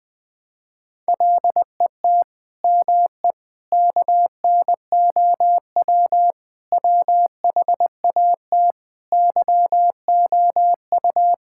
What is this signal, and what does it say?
Morse code, heard as a keyed tone, about 20 wpm, 700 Hz. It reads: LET ME KNOW WHAT YOU
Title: Sentences with 9-Word Phrases - 20wpm